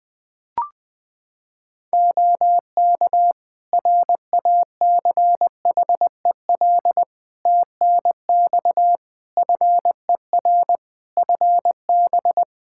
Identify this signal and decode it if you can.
Morse code, heard as a keyed tone, about 21 wpm, 700 Hz. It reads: OK RACHEL TNX FER FB
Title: K1USN SST QSO Practice 20wpm